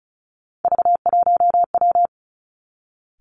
Morse code, heard as a keyed tone, about 36 wpm, 700 Hz. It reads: V1W
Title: Three Challenging Letter-Number Combinations 35wpm